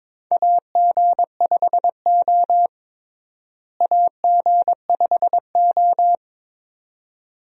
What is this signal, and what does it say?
Morse code, heard as a keyed tone, about 22 wpm, 700 Hz. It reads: AG5O AG5O
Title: USA Call Signs - 4 Characters - Repeated 2x - 22wpm